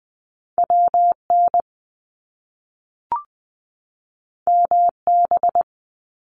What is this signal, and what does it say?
Morse code, heard as a keyed tone, about 20 wpm, 700 Hz. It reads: WN MB
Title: Top Two Letter Combinations in Words 20wpm